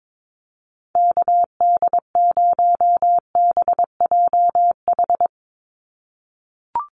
Morse code, heard as a keyed tone, about 22 wpm, 700 Hz. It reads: KD0BJH
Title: USA Call Signs 22wpm